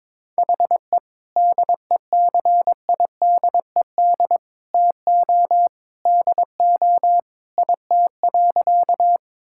Morse code, heard as a keyed tone, about 22 wpm, 700 Hz. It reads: HE DECIDED TO DO IT.
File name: practice / 22wpm / Sentences with 3-Word Phrases - 22wpm